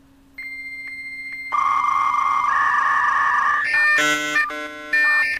Dial-up modem handshake.